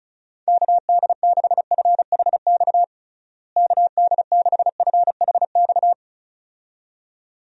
Morse code, heard as a keyed tone, about 34 wpm, 700 Hz. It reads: KD6FHX KD6FHX
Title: USA Call Signs - 6 Characters - Repeated 2x - 35wpm